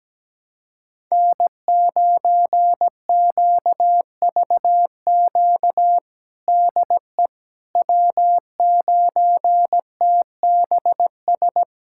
Morse code, heard as a keyed tone, about 17 wpm, 700 Hz. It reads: N9QVQ DE W9TBS
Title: USA Callsign DE Callsign - 5 Characters - 17wpm